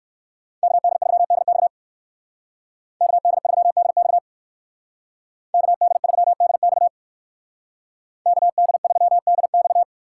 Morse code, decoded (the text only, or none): KD3DX KD3DX KD3DX KD3DX